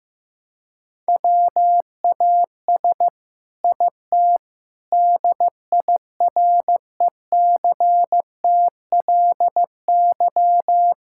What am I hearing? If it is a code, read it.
Morse code, heard as a keyed tone, about 15 wpm, 700 Hz. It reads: WAS IT DIRECTLY
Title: Sentences from Top 1000 Words - Review Part 5 - 15wpm